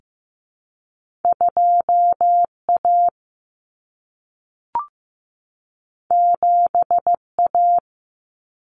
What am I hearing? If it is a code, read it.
Morse code, heard as a keyed tone, about 15 wpm, 700 Hz. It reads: 2A 7A
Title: Common Cut Numbers - 2-Digit 15wpm